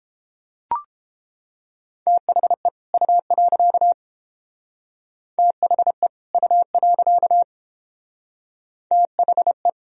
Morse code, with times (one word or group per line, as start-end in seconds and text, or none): 2.1-2.7 THE
2.9-3.9 U.
5.4-6.1 THE
6.3-7.4 U.
8.9-9.7 THE